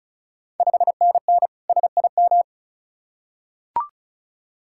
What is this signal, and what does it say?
Morse code, heard as a keyed tone, about 36 wpm, 690 Hz. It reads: 5NN SIM
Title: Ontario QSO Party Exchange 35wpm